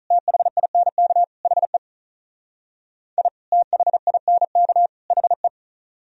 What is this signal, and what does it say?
Morse code, heard as a keyed tone, about 46 wpm, 700 Hz. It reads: THINK HE I THINK HE